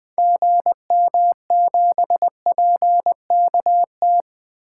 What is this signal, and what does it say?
Morse code, heard as a keyed tone, about 20 wpm, 700 Hz. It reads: GM7PKT
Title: 2023 SOTA Activator Call Signs - 20wpm